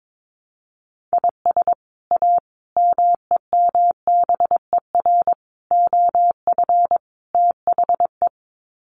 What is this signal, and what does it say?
Morse code, heard as a keyed tone, about 22 wpm, 700 Hz. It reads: IS A MEMBER OF THE
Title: Sets of 5 Words 22wpm